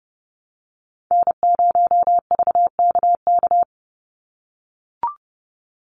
Morse code, heard as a keyed tone, about 30 wpm, 700 Hz. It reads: N0VKK
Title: USA Call Signs 30wpm